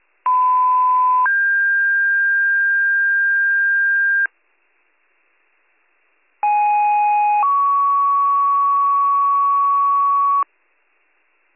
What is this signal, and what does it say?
Two-tone page, heard as a one-second tone, then a three-second tone, about 1010 then 1600 Hz, about 830 then 1120 Hz. It is heard from 0.2 to 10.4 s.